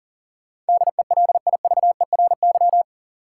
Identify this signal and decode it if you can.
Morse code, heard as a keyed tone, about 37 wpm, 700 Hz. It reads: DELIVERY
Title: Sentences from Top 1500 Words - New Word Drill - 40wpm